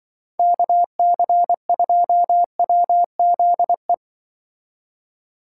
Morse code, heard as a keyed tone, about 25 wpm, 700 Hz. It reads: KC2WZE
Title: Speed Racing - USA 2x3 Call Signs 20wpm